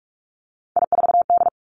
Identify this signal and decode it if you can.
Morse code, heard as a keyed tone, about 46 wpm, 700 Hz. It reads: I4D
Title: Three Challenging Letter-Number Combinations 45wpm